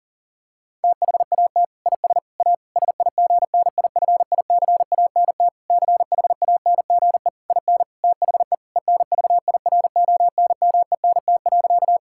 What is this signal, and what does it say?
Morse code, heard as a keyed tone, about 37 wpm, 700 Hz. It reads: THAT IS A SIGNIFICANT CHANGE IN THE ENVIRONMENT.
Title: Sentences from Top 1000 Words - Review Part 6 - 40wpm